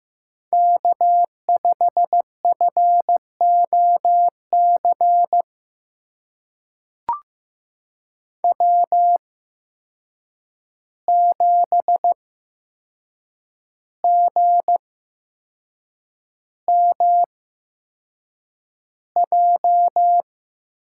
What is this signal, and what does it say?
Morse code, heard as a keyed tone, about 15 wpm, spaced as for 6 wpm, 700 Hz. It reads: K5FOC W7GMJ
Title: USA Call Signs - 5 Characters - 8x Character Spacing - 15wpm